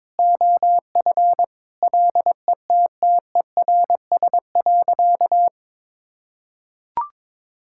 Morse code, heard as a keyed tone, about 22 wpm, 700 Hz. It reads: OF LETTERS.
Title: Sentences from Top 1000 Words - Review Part 6 - 22wpm